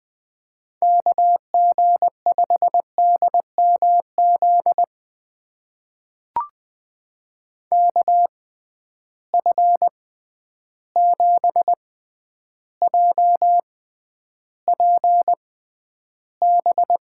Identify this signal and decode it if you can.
Morse code, heard as a keyed tone, about 20 wpm, spaced as for 10 wpm, 700 Hz. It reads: KG5DMZ KF7JPB
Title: USA Call Signs - 6 Characters - 6x Character Spacing - 20wpm